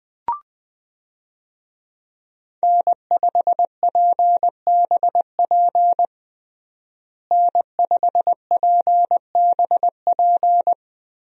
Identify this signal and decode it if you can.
Morse code, heard as a keyed tone, about 21 wpm, 700 Hz. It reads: N5PBP N5PBP
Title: USA Call Signs - 5 Characters - Repeated 3x - 20wpm